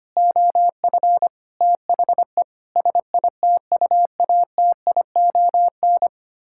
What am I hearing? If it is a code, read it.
Morse code, heard as a keyed tone, about 25 wpm, 700 Hz. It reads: OF THE SITUATION
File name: Sets of 9 Words - 25wpm